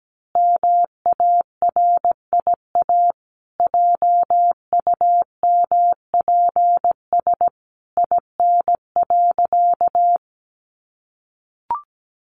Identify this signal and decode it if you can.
Morse code, heard as a keyed tone, about 17 wpm, 700 Hz. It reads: MARIA JUMPS IN.
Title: Easy Phrases 17wpm